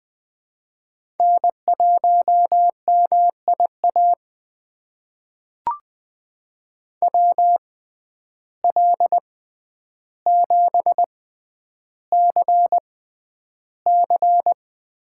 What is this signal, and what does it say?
Morse code, heard as a keyed tone, about 20 wpm, spaced as for 10 wpm, 700 Hz. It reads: N1MIA WL7CC
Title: USA Call Signs - 5 Characters - 6x Character Spacing - 20wpm